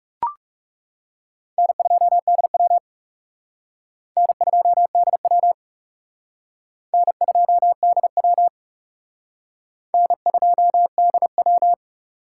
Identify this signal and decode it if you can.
Morse code, heard as a keyed tone, about 36 wpm, 700 Hz. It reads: N2DW N2DW N2DW N2DW
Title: Speed Racing - USA 1x2 Call Signs 30wpm